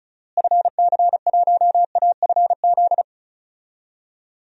Morse code, heard as a keyed tone, about 36 wpm, 700 Hz. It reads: FC1AFZ
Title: DXCC-100 Call Signs - 6 Characters - Repeated 2x - 35wpm